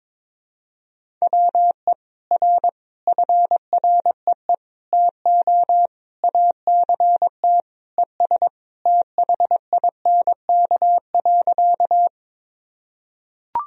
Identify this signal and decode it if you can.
Morse code, heard as a keyed tone, about 21 wpm, 700 Hz. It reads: WE R FREE TO ACT ES THINK.
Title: Sentences from Top 500 Words and Abbreviations 22wpm